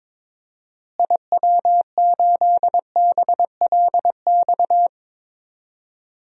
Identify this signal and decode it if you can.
Morse code, heard as a keyed tone, about 21 wpm, 700 Hz. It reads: IW8BLX
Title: DXCC-100 Call Signs - 6 Characters - Repeated 3x - 22wpm